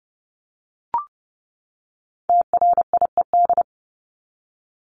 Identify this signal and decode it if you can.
Morse code, heard as a keyed tone, about 30 wpm, 700 Hz. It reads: TRIED
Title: ED Suffix - Top 100 Words with Base Words 30wpm